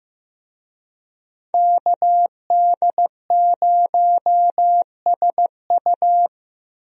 Morse code, heard as a keyed tone, about 15 wpm, 700 Hz. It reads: KD0SU
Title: USA Call Signs - 5 Characters - Repeated 2x - 15wpm